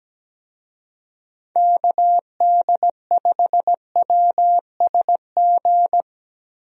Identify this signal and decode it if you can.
Morse code, heard as a keyed tone, about 17 wpm, 700 Hz. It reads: KD5WSG
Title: USA Call Signs - 6 Characters - Repeated 3x - 17wpm